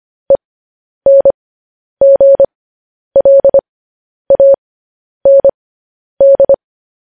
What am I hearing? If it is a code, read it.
Morse code, heard as a keyed tone, about 25 wpm, spaced as for 10 wpm, 550 Hz. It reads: ENGLAND
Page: News Headlines in Morse Code at 10 WPM